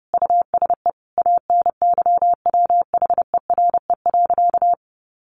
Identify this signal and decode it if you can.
Morse code, heard as a keyed tone, about 30 wpm, 700 Hz. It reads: USE ANYWHERE.
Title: Sentences from Top 1400 Words 30wpm